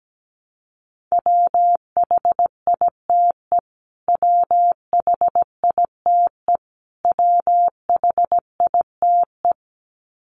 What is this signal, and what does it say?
Morse code, heard as a keyed tone, about 17 wpm, 700 Hz. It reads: WHITE WHITE WHITE
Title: Top 500 Words Repeated 17wpm